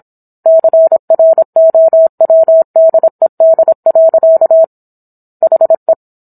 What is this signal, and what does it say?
Morse code, heard as a keyed tone, about 27 wpm, 650 Hz. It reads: CROWDED. HE